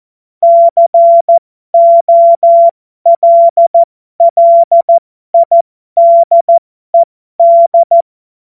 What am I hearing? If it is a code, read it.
Morse code, heard as a keyed tone, about 14 wpm, spaced as for 12 wpm, 680 Hz. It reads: COLLIDED
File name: Morsekurs / Witze